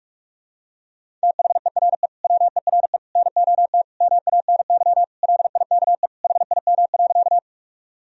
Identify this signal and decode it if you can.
Morse code, heard as a keyed tone, about 46 wpm, 700 Hz. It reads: THERE WERE NOT MANY LIKE HIM.
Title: Sentences from Top 1000 Words - Review Part 1 - 45wpm